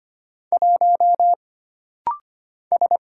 Morse code, heard as a keyed tone, about 25 wpm, 700 Hz. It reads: 1 S
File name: Single Character (Letter:Number:Punctuation:Prosign) - Rapid-Fire - 25wpm